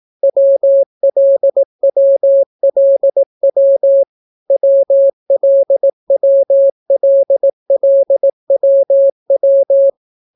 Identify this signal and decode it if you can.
Morse code, heard as a keyed tone, about 18 wpm, 550 Hz. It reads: WLWLW WLWLLWW